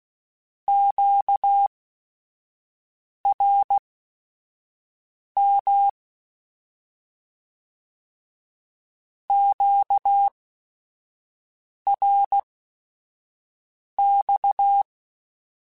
Morse code, heard as a keyed tone, about 16 wpm, spaced as for 5 wpm, 800 Hz. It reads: QRM QRX
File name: Q Calls / Q Calls-mp3s / Q Calls Once @5